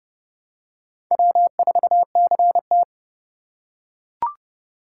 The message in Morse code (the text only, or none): W4CT